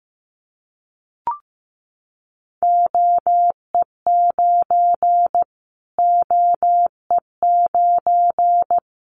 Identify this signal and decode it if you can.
Morse code, heard as a keyed tone, about 15 wpm, 700 Hz. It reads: OE9 OE9
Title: Callsign Prefixes for Top 25 DXCC - Repeated 3x - 15wpm